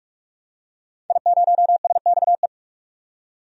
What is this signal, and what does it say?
Morse code, heard as a keyed tone, about 46 wpm, 700 Hz. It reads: I0SXE
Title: DXCC-100 Call Signs - 5 Characters - 45wpm